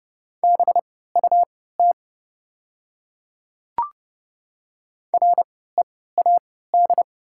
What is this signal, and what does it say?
Morse code, heard as a keyed tone, about 30 wpm, spaced as for 17 wpm, 700 Hz. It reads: BUT READ
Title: 3x Character Spacing - Top 300 Words - 30wpm